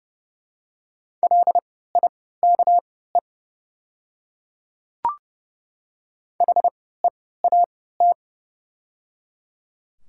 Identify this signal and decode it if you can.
Morse code, heard as a keyed tone, about 31 wpm, spaced as for 17 wpm, 700 Hz. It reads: LIKE HEAT
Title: 3x Character Spacing - Top 500 Words - 30wpm